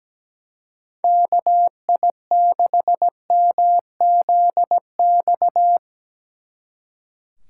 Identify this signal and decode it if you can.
Morse code, heard as a keyed tone, about 18 wpm, 700 Hz. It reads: KI6MZX
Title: Speed Racing - USA 2x3 Call Signs 17wpm